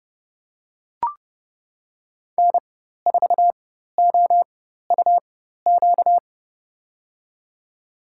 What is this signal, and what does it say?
Morse code, heard as a keyed tone, about 29 wpm, spaced as for 14 wpm, 690 Hz. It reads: N4OUQ
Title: USA Call Signs - 4x Character Spacing - 30wpm